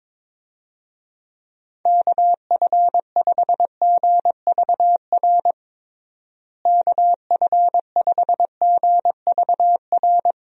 Morse code, heard as a keyed tone, about 22 wpm, 700 Hz. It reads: KF5GVR KF5GVR
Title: USA Call Signs - 6 Characters - Repeated 3x - 22wpm